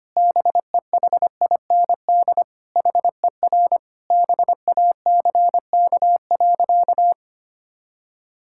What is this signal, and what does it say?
Morse code, heard as a keyed tone, about 25 wpm, 700 Hz. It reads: BEHIND HER BACK.
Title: Sentences from Top 1000 Words - Review Part 6 - 25wpm